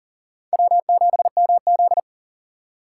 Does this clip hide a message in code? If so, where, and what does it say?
Yes: Morse code from 0.5 to 2.0 s, heard as a keyed tone, about 52 wpm, 700 Hz. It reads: W7MZ